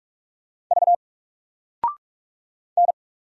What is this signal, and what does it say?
Morse code, heard as a keyed tone, about 44 wpm, 700 Hz. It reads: V N
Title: Single Letter-Number 45wpm